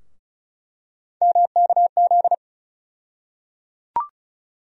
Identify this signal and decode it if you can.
Morse code, heard as a keyed tone, about 36 wpm, 700 Hz. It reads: MKZ
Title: USA Call Sign Postfixes - 3 Letters 35wpm